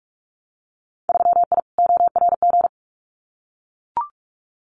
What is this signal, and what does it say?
Morse code, heard as a keyed tone, about 44 wpm, 700 Hz. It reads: 3I ORG
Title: Winter Field Day Exchanges Encore 45wpm